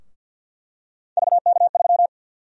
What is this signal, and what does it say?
Morse code, heard as a keyed tone, about 48 wpm, 700 Hz. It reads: VK3
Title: USA Call Sign Prefixes - 50wpm